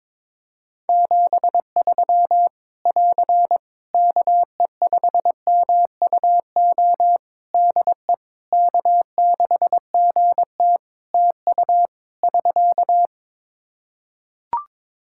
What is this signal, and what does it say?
Morse code, heard as a keyed tone, about 22 wpm, 700 Hz. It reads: 73 <AR> KE5MUO DE K6GT TU <SK>
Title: Practice QSO - Exchange 4 - Ending - 22wpm